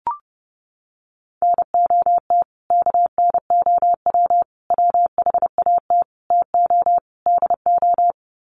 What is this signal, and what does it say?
Morse code, heard as a keyed tone, about 30 wpm, 700 Hz. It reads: NOT KNOW WHAT TO DO
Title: Sets of 5 Words 30wpm